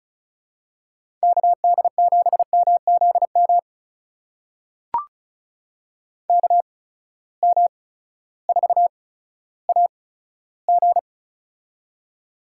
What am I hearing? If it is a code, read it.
Morse code, heard as a keyed tone, about 36 wpm, spaced as for 16 wpm, 700 Hz. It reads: KD7MZM KM4AG
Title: USA Call Signs - 8x Character Spacing - 35wpm